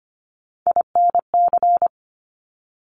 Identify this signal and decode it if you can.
Morse code, heard as a keyed tone, about 25 wpm, 700 Hz. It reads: INC
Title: Top Three Letter Combinations in Words 25wpm